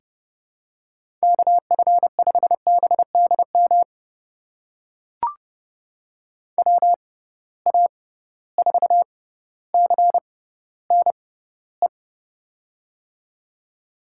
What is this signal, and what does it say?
Morse code, heard as a keyed tone, about 31 wpm, spaced as for 16 wpm, 700 Hz. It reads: KF5BDM WA4CNE
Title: USA Call Signs - 6 Characters - 6x Character Spacing - 30wpm